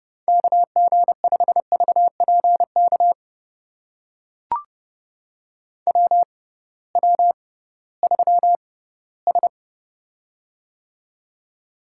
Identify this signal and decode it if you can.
Morse code, heard as a keyed tone, about 30 wpm, spaced as for 17 wpm, 700 Hz. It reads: KG5VPK WW3S
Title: USA Call Signs - 6x Character Spacing - 30wpm